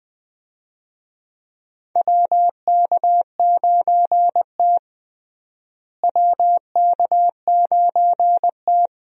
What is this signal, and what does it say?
Morse code, heard as a keyed tone, about 20 wpm, 700 Hz. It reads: WK9T WK9T
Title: USA Call Signs - 4 Characters - Repeated 2x - 20wpm